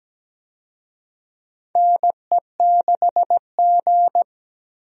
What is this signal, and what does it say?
Morse code, heard as a keyed tone, about 17 wpm, 700 Hz. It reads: NE6G
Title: USA Call Signs - 4 Characters - 17wpm